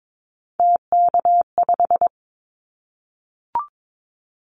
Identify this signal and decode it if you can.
Morse code, heard as a keyed tone, about 22 wpm, 700 Hz. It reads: TK5
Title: Callsign Prefixes for Top 100 DXCC - 22wpm